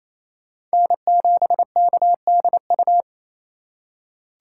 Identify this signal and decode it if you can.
Morse code, heard as a keyed tone, about 28 wpm, 700 Hz. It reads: N7KDU
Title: USA Call Signs - 5 Characters - 2x Character Spacing - 28wpm